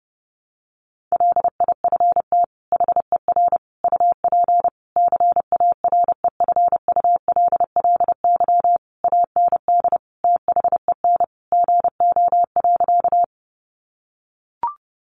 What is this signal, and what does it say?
Morse code, heard as a keyed tone, about 30 wpm, 700 Hz. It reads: LIFT HER UP CAREFULLY AND THEN GO.
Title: Sentences from Top 1300 Words 30wpm